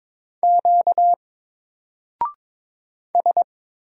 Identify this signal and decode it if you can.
Morse code, heard as a keyed tone, about 22 wpm, 700 Hz. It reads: Q S
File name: Single-Letter-Number-22wpm